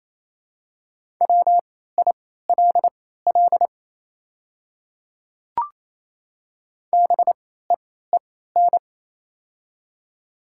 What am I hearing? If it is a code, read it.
Morse code, heard as a keyed tone, about 27 wpm, spaced as for 16 wpm, 700 Hz. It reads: WILL BEEN
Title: 3x Character Spacing - Top 100 Words - 28wpm